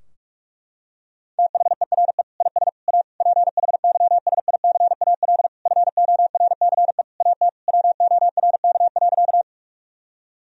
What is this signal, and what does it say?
Morse code, heard as a keyed tone, about 47 wpm, 700 Hz. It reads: THERE IS A PHYSICAL FORCE AT WORK.
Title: Sentences from Top 1000 Words - Review Part 4 - 45wpm